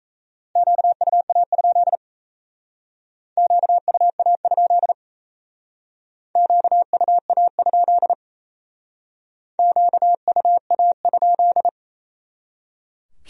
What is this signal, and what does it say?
Morse code, heard as a keyed tone, about 36 wpm, 700 Hz. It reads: QUA? QUA? QUA? QUA?